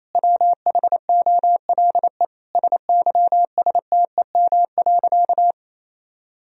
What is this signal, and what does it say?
Morse code, heard as a keyed tone, about 28 wpm, 700 Hz. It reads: WHOLE SYSTEM.
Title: Sentences from Top 1000 Words - Review Part 3 - 28wpm